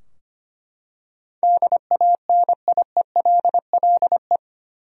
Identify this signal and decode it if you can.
Morse code, heard as a keyed tone, about 25 wpm, 700 Hz. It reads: DANIELLE
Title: USA Common Names - 25wpm